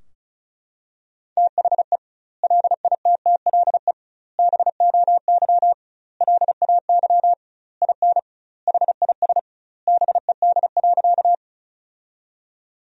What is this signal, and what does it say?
Morse code, heard as a keyed tone, about 36 wpm, 700 Hz. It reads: THE LITTLE BOY LAY IN HIS BED.
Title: 2x Word-Spacing - Sentences from Top 500 Words - 35wpm